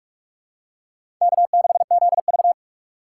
Morse code, heard as a keyed tone, about 44 wpm, 700 Hz. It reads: K6ZV
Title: USA Call Signs - 2x Character Spacing - 45wpm